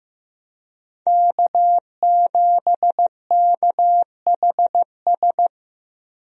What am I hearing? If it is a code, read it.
Morse code, heard as a keyed tone, about 15 wpm, 700 Hz. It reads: K7KHS
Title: USA Call Signs - 5 Characters - 15wpm